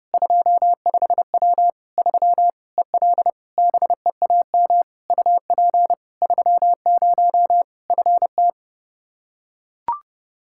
Morse code, heard as a keyed tone, about 30 wpm, 700 Hz. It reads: 25W 3 EL BEAM UP 30 FT
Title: Practice QSO - Exchange 2 - Info - 30wpm